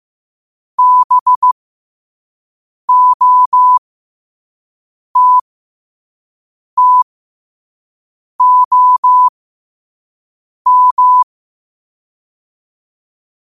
Morse code, heard as a keyed tone, about 15 wpm, spaced as for 5 wpm, 1000 Hz. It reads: BOTTOM
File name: 5wpm